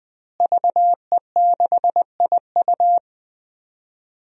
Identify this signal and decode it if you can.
Morse code, heard as a keyed tone, about 20 wpm, 700 Hz. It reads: VE6IU
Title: DXCC-100 Call Signs - 5 Characters - Repeated 3x - 20wpm